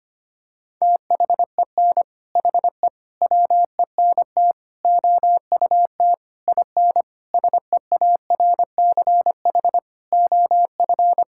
Morse code, heard as a keyed tone, about 25 wpm, 700 Hz. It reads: THEN HE WENT OUT IN SEARCH OF
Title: Sentences from Top 1000 Words - Review Part 5 - 25wpm